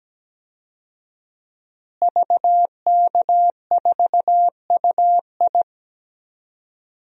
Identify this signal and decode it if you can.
Morse code, heard as a keyed tone, about 17 wpm, 700 Hz. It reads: VK4UI
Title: DXCC-100 Call Signs - 5 Characters - Repeated 2x - 17wpm